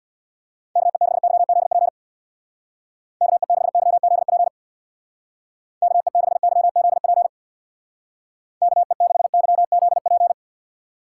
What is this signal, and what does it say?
Morse code, heard as a keyed tone, about 62 wpm, 700 Hz. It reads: KE6YZP KE6YZP KE6YZP KE6YZP
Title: Speed Racing - USA 2x3 Call Signs 50wpm